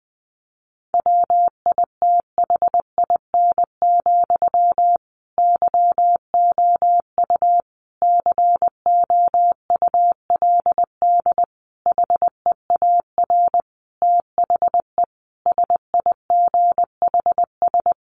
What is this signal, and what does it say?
Morse code, heard as a keyed tone, about 20 wpm, 700 Hz. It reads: WITHIN, YOU COULD HEAR THE SIGHS
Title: A Sound Of Thunder 20wpm